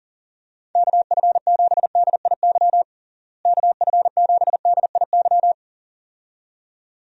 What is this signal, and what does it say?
Morse code, heard as a keyed tone, about 37 wpm, 700 Hz. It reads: KF7DIY KF7DIY
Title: USA Call Signs - 6 Characters - Repeated 2x - 40wpm